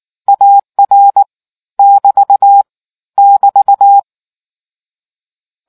Morse code, heard as a keyed tone, about 19 wpm, 800 Hz. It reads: AR <BT> <BT>